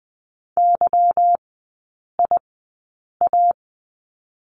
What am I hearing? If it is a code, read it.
Morse code, heard as a keyed tone, about 20 wpm, spaced as for 8 wpm, 700 Hz. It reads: YIA
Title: 205 Lesson 33e - Letter j - Three Characters 20wpm